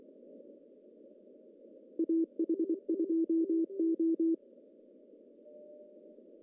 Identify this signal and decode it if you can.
Morse code, heard as a keyed tone, about 23 wpm, 310 Hz. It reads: AH2O
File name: CW / cqww-cw-k5tr